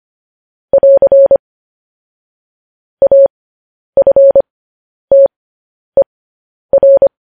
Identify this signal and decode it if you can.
Morse code, heard as a keyed tone, about 25 wpm, spaced as for 10 wpm, 550 Hz. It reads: <AR> AFTER
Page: news headlines in morse code at 10 wpm